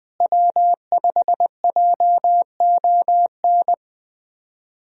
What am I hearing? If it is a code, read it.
Morse code, heard as a keyed tone, about 20 wpm, 700 Hz. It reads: W5JON
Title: USA Call Signs - 5 Characters - Repeated 2x - 20wpm